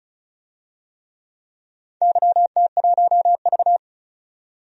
Morse code, heard as a keyed tone, about 34 wpm, 700 Hz. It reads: YT1V
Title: DXCC-100 Call Signs - 4 Characters - 35wpm